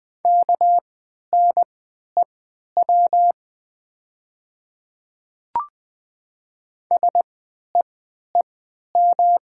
Morse code, heard as a keyed tone, about 20 wpm, spaced as for 11 wpm, 700 Hz. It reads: KNEW SEEM
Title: 3x Character Spacing - Top 500 Words - 20wpm